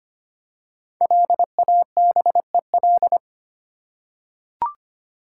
Morse code, heard as a keyed tone, about 25 wpm, 700 Hz. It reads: LABEL
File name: 5 Letter Words - First 3 Letters Given - 25wpm